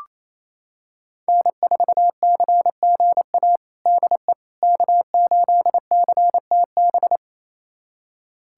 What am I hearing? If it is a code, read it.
Morse code, heard as a keyed tone, about 28 wpm, 700 Hz. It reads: N4CGA DE K8CTB
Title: USA Callsign DE Callsign - 5 Characters - 28wpm